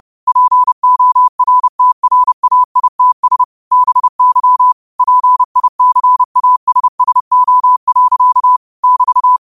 Morse code, heard as a keyed tone, about 30 wpm, 1000 Hz. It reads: PORTRAITS BY PICASSO. <BT>
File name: qotdfiles / 30wpm